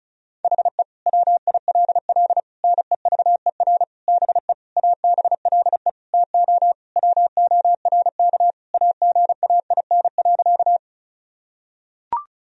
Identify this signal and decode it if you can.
Morse code, heard as a keyed tone, about 34 wpm, 700 Hz. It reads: HE WILL NEVER BE ABLE TO WORK AGAIN.
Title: Sentences from Top 1000 Words - Review Part 2 - 35wpm